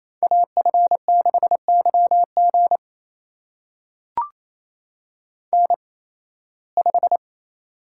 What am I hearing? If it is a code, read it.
Morse code, heard as a keyed tone, about 28 wpm, spaced as for 18 wpm, 700 Hz. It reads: AF6YG N5
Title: USA Call Signs - 5 Characters - 8x Character Spacing - 28wpm